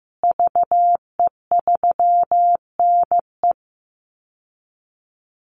Morse code, heard as a keyed tone, about 15 wpm, 700 Hz. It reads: VE3NE
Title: Call Signs Repeated 2x - 15wpm